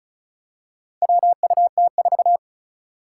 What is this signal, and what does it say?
Morse code, heard as a keyed tone, about 36 wpm, 700 Hz. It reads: WUT4
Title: Four Challenging Letter-Number Combinations 35wpm